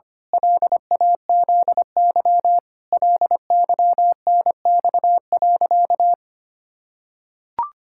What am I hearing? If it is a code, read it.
Morse code, heard as a keyed tone, about 25 wpm, 700 Hz. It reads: LAZY LYNX.
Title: Pangram Challenge v2 - 25wpm